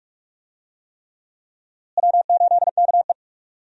Morse code, heard as a keyed tone, about 46 wpm, 700 Hz. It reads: W8KE
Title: USA Call Signs - 4 Characters - Repeated 3x - 45wpm